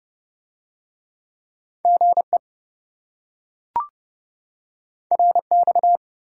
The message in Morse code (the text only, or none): GE RX